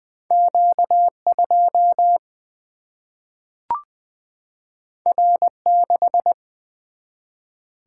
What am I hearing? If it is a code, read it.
Morse code, heard as a keyed tone, about 20 wpm, 700 Hz. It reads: Q2 R6
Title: Single-Letter Single-Number 20wpm